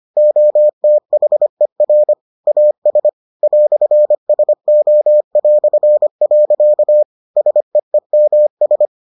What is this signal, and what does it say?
Morse code, heard as a keyed tone, about 25 wpm, 600 Hz. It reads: OTHER AS "SO". SEEMS